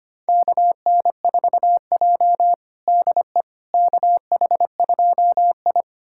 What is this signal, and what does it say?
Morse code, heard as a keyed tone, about 25 wpm, 700 Hz. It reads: KN4J DE KH2I
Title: USA Callsign DE Callsign - 4 Characters - 25wpm